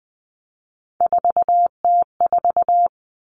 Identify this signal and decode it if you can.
Morse code, heard as a keyed tone, about 20 wpm, 700 Hz. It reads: 4T4
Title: Common Cut Numbers - 3-Digit 20wpm